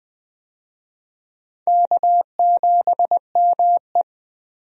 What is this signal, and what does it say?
Morse code, heard as a keyed tone, about 20 wpm, 700 Hz. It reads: K7ME
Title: USA Call Signs - 4 Characters - 20wpm